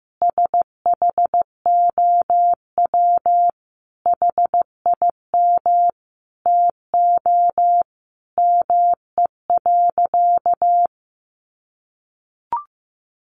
Morse code, heard as a keyed tone, about 15 wpm, 700 Hz. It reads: SHOW HIM TO ME.
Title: Sentences from Top 200 Words 15wpm